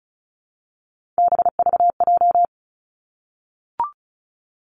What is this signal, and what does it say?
Morse code, heard as a keyed tone, about 36 wpm, 700 Hz. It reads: BVJ
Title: Three Challenging Letter-Number Combinations 35wpm